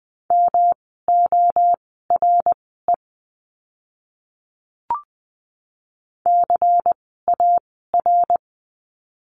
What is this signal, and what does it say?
Morse code, heard as a keyed tone, about 20 wpm, spaced as for 15 wpm, 700 Hz. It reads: MORE CAR
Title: 108 Lesson 18f - Letter m - Words 20wpm